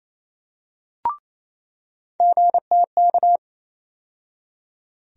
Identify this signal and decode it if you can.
Morse code, heard as a keyed tone, about 28 wpm, 700 Hz. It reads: GTK
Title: USA Call Sign Postfixes - 3 Letters 28wpm